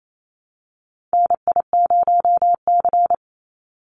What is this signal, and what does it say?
Morse code, heard as a keyed tone, about 28 wpm, 700 Hz. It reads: NI0C
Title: Call Signs Repeated 3x - 28wpm